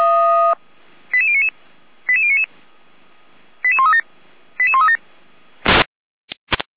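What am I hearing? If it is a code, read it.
Five-tone selective call (ZVEI), heard as quick runs of five tones, tones of about 70 ms. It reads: 88099 88099 88137 88137
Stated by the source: ZVEI five-tone call received with an SDR.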